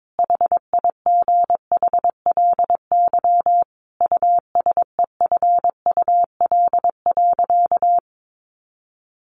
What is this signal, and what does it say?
Morse code, heard as a keyed tone, about 22 wpm, 700 Hz. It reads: HIGHLY USEFUL.
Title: Sentences from Top 1100 Words 22wpm